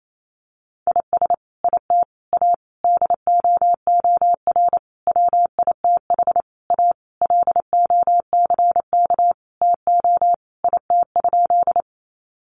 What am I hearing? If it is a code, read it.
Morse code, heard as a keyed tone, about 28 wpm, 700 Hz. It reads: IS IT A DOOR WITH A LOCK TO IT?
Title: Sentences from Top 1300 Words 28wpm